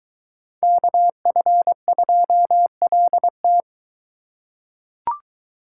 Morse code, heard as a keyed tone, about 20 wpm, 700 Hz. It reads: KF2LT